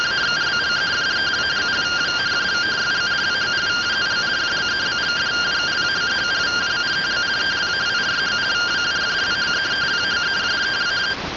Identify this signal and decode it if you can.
Two-tone teleprinter signal, about 170 Hz shift.